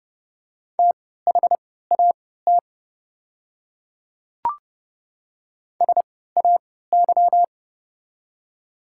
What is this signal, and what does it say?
Morse code, heard as a keyed tone, about 30 wpm, spaced as for 17 wpm, 700 Hz. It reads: THAT SAY
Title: 3x Character Spacing - Top 200 Words - 30wpm